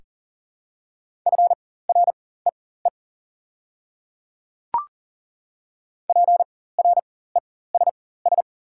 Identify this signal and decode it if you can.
Morse code, heard as a keyed tone, about 40 wpm, spaced as for 19 wpm, 700 Hz. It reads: FREE PRESS
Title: 4x Character Spacing - Top 500 Words - 40wpm